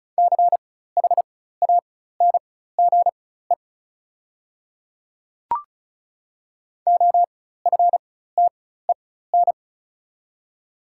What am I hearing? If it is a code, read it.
Morse code, heard as a keyed tone, about 36 wpm, spaced as for 16 wpm, 690 Hz. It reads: CHANGE OFTEN
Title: 4x Character Spacing - Top 300 Words - 35wpm